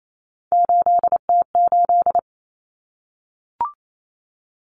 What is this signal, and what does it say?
Morse code, heard as a keyed tone, about 28 wpm, 700 Hz. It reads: 8T8
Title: Common Cut Numbers - 3-Digit 28wpm